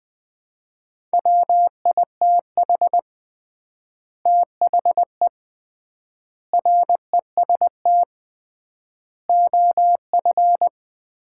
Morse code, heard as a keyed tone, about 20 wpm, 700 Hz. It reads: WITH THE REST OF
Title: 3x Word-Spacing - Sets of 5 Words - 20wpm